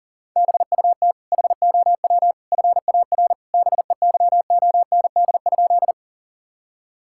Morse code, heard as a keyed tone, about 40 wpm, 700 Hz. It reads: BUT HOW FAR BEYOND?
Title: Sentences from Top 1000 Words - Review Part 6 - 40wpm